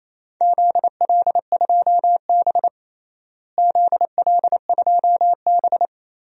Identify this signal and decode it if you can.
Morse code, heard as a keyed tone, about 28 wpm, 700 Hz. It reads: ZL2B ZL2B
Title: DXCC-100 Call Signs - 4 Characters - Repeated 2x - 28wpm